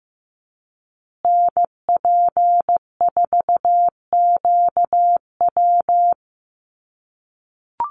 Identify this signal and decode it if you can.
Morse code, heard as a keyed tone, about 15 wpm, 700 Hz. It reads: NP4QW
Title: USA Call Signs Repeated 3x 15wpm